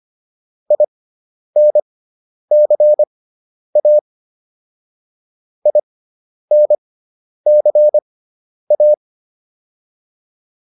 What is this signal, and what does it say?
Morse code, heard as a keyed tone, about 25 wpm, spaced as for 10 wpm, 600 Hz. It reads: INCA INCA